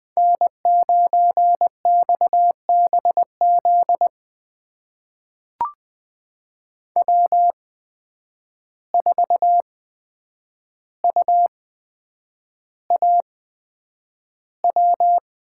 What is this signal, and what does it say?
Morse code, heard as a keyed tone, about 20 wpm, spaced as for 9 wpm, 700 Hz. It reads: N9XBZ W4UAW
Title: USA Call Signs - 5 Characters - 8x Character Spacing - 20wpm